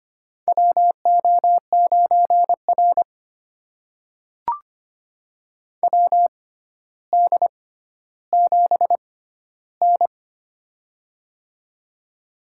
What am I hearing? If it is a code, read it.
Morse code, heard as a keyed tone, about 25 wpm, spaced as for 13 wpm, 700 Hz. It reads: WO9R WD7N
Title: USA Call Signs - 4 Characters - 6x Character Spacing - 25wpm